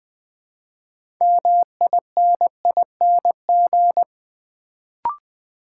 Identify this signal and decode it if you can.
Morse code, heard as a keyed tone, about 20 wpm, 700 Hz. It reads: MINING
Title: Build A Word - 2 Letters At A Time - 6-Letter Words - 20wpm